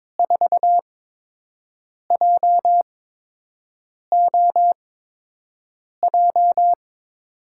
Morse code, heard as a keyed tone, about 22 wpm, spaced as for 6 wpm, 700 Hz. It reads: 4JOJ
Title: USA Call Signs - 6 Characters - 8x Character Spacing - 22wpm